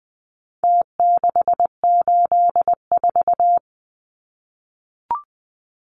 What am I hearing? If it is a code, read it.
Morse code, heard as a keyed tone, about 20 wpm, 700 Hz. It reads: T684
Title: Numbers - Contest Style - 4-Digit 20wpm